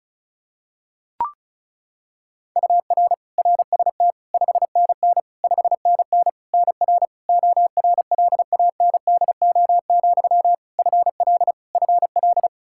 Morse code, heard as a keyed tone, about 34 wpm, 700 Hz. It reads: UR RST 5NN 5NN NR ORLANDO, FL FL
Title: Practice QSO - Exchange 1 - Beginning - 35wpm